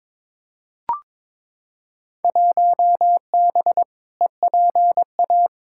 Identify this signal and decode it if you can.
Morse code, heard as a keyed tone, about 22 wpm, 700 Hz. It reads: 1B EPA
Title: Field Day Exchanges Encore - 22wpm